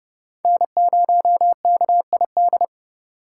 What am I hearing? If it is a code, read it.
Morse code, heard as a keyed tone, about 30 wpm, 700 Hz. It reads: N0KID